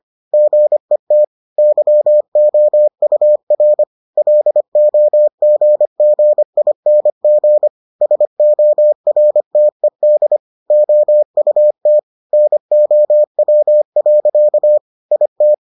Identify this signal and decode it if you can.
Morse code, heard as a keyed tone, about 25 wpm, 600 Hz. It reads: GET YOUR LOGGING SORTED OUT NOW. IT